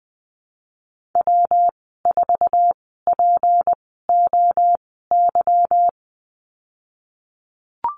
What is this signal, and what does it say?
Morse code, heard as a keyed tone, about 19 wpm, spaced as for 14 wpm, 700 Hz. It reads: W4POY
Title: 154 Lesson 25g - Letter p - Call Signs 20wpm